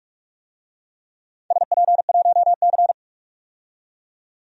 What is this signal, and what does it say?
Morse code, heard as a keyed tone, about 44 wpm, 700 Hz. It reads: SP1C
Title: DXCC-100 Call Signs - 4 Characters - 45wpm